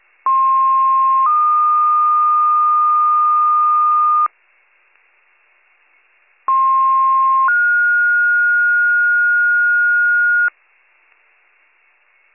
Two-tone page, heard as a one-second tone, then a three-second tone, about 1060 then 1220 Hz, about 1060 then 1430 Hz. It is heard from 0.3 to 10.5 s.